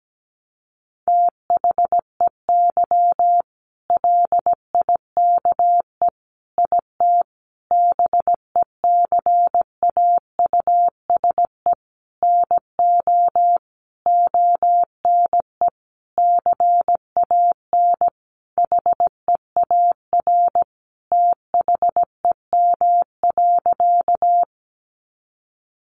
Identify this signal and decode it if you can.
Morse code, heard as a keyed tone, about 17 wpm, 700 Hz. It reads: THEY LIKE IT BECAUSE NO ONE CAN HEAR THEM.
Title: Sentences from Top 600 Words 17wpm